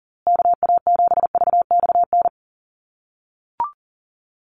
Morse code, heard as a keyed tone, about 37 wpm, 700 Hz. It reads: KA7VXN
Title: USA Call Signs Repeated 3x 40wpm